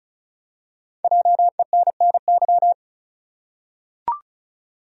Morse code, heard as a keyed tone, about 34 wpm, 700 Hz. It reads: JENNY
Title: USA Common Names - 35wpm